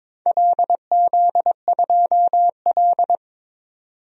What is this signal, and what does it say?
Morse code, heard as a keyed tone, about 22 wpm, 700 Hz. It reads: LZ2L
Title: DXCC-100 Call Signs - 4 Characters - 22wpm